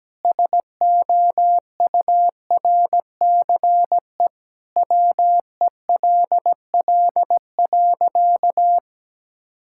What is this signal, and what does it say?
Morse code, heard as a keyed tone, about 17 wpm, 700 Hz. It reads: SOURCE WELL.
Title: Sentences from Top 1000 Words - Review Part 5 - 17wpm